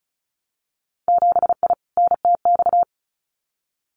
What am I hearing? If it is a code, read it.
Morse code, heard as a keyed tone, about 36 wpm, 700 Hz. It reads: 7I NTX
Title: Winter Field Day Exchanges Encore 35wpm